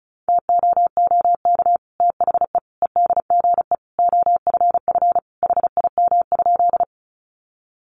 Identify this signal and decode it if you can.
Morse code, heard as a keyed tone, about 34 wpm, 700 Hz. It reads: TOOK THE EDGE OFF HIM?
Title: Sentences from Top 1300 Words 35wpm